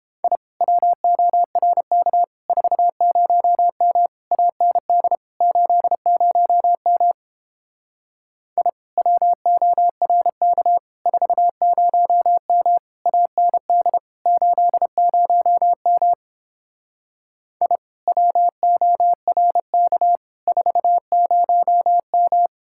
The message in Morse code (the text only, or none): I WORK 40M AND 80M I WORK 40M AND 80M I WORK 40M